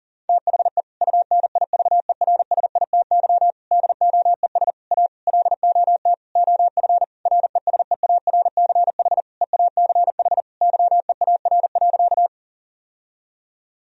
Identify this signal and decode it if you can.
Morse code, heard as a keyed tone, about 40 wpm, 700 Hz. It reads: THE UNIVERSITY DOES A LOT OF RESEARCH EACH YEAR.
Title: Sentences from Top 1000 Words - Review Part 4 - 40wpm